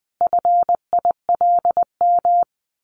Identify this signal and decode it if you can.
Morse code, heard as a keyed tone, about 20 wpm, 700 Hz. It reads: FILM
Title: A Sound Of Thunder 20wpm